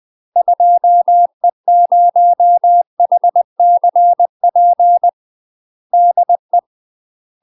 Morse code, heard as a keyed tone, about 20 wpm, 700 Hz. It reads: 2E0HCP DE
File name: UK MASTER QSOs TRAINING / 20WPM / UK MASTER QSOs 0014